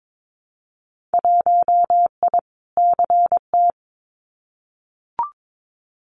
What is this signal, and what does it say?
Morse code, heard as a keyed tone, about 22 wpm, 700 Hz. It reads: 1I CT
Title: Winter Field Day Exchanges 22wpm